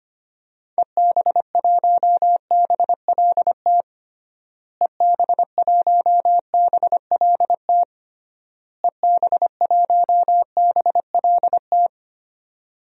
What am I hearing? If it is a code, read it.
Morse code, heard as a keyed tone, about 25 wpm, 700 Hz. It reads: EB1BLT EB1BLT EB1BLT
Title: DXCC-100 Call Signs - 6 Characters - Repeated 3x - 25wpm